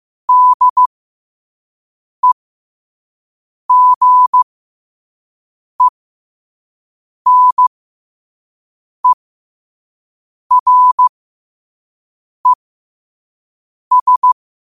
Morse code, heard as a keyed tone, about 15 wpm, spaced as for 5 wpm, 1000 Hz. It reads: DEGENERES
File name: qotdfiles / 5wpm